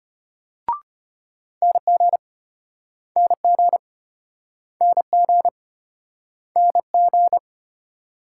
Morse code, heard as a keyed tone, about 34 wpm, 700 Hz. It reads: NG NG NG NG